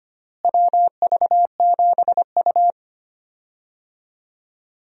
Morse code, heard as a keyed tone, about 25 wpm, 700 Hz. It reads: WV7U
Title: USA Call Signs - 4 Characters - Repeated 3x - 25wpm